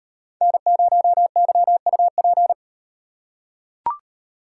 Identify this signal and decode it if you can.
Morse code, heard as a keyed tone, about 38 wpm, 700 Hz. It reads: N0YUP